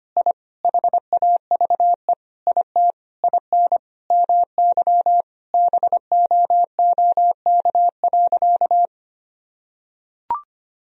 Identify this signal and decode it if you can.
Morse code, heard as a keyed tone, about 25 wpm, 700 Hz. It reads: I HAVE IT IN MY BOOK.
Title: Sentences from Top 1000 Words - Review Part 1 - 25wpm